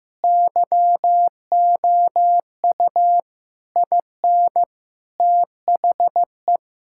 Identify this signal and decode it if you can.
Morse code, heard as a keyed tone, about 15 wpm, 700 Hz. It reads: YOU IN THE
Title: Sentences from Top 1600 Words 15wpm